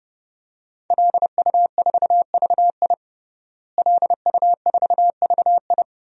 Morse code, heard as a keyed tone, about 32 wpm, 700 Hz. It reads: LU4VI LU4VI
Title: DXCC-100 Call Signs - 5 Characters - Repeated 2x - 30wpm